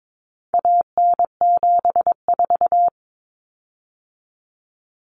Morse code, heard as a keyed tone, about 22 wpm, 700 Hz. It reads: AN74
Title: Common Cut Numbers - 4-Digit 22wpm